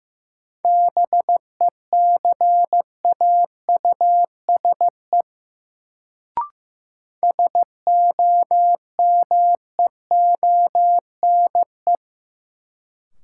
Morse code, heard as a keyed tone, about 15 wpm, 700 Hz. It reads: BECAUSE SOMEONE
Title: Sentences from Top 600 Words - New Word Drill - 15wpm